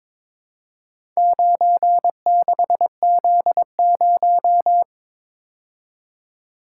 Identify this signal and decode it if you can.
Morse code, heard as a keyed tone, about 23 wpm, 700 Hz. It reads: 96Z0
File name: practice / 22wpm / Four Letter-Number Combinations - 22wpm